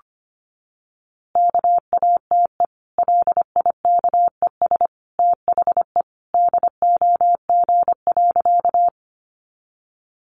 Morse code, heard as a keyed tone, about 25 wpm, 700 Hz. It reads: KATE LIKES THE DOG.
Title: Easy Phrases 25wpm